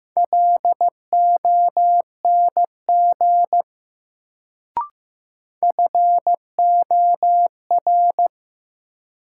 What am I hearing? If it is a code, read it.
Morse code, heard as a keyed tone, about 15 wpm, 700 Hz. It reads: LONG FOR
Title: Sentences from Top 100 Words - New Word Drill - 15wpm